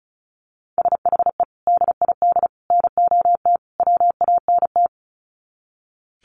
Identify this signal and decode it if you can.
Morse code, heard as a keyed tone, about 34 wpm, 700 Hz. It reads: SHE DID NOT WANT
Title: Sets of 4 Words 35wpm